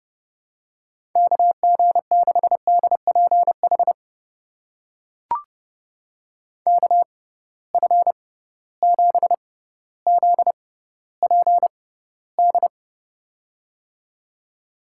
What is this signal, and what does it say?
Morse code, heard as a keyed tone, about 30 wpm, spaced as for 15 wpm, 700 Hz. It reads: KG6DPH KF7ZPD
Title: USA Call Signs - 6 Characters - 6x Character Spacing - 30wpm